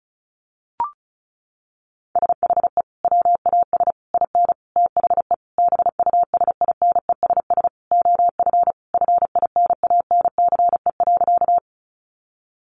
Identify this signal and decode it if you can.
Morse code, heard as a keyed tone, about 34 wpm, 700 Hz. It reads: SHE WAS IN THE BUSINESS OF FINANCE.
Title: Sentences from Top 1500 Words 35wpm